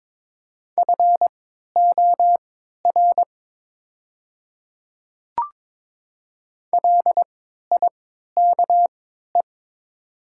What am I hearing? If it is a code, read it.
Morse code, heard as a keyed tone, about 22 wpm, spaced as for 13 wpm, 700 Hz. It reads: FOR LIKE
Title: 3x Character Spacing - Top 100 Words - 22wpm